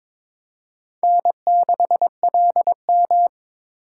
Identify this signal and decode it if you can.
Morse code, heard as a keyed tone, about 22 wpm, 700 Hz. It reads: N6LM
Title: USA Call Signs - 4 Characters - 3x Character Spacing - 22wpm